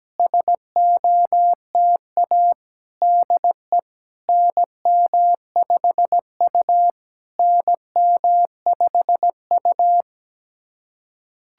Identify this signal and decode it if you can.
Morse code, heard as a keyed tone, about 17 wpm, 700 Hz. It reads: SOTA DE NM5U NM5U
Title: SOTA QSO Practice 17wpm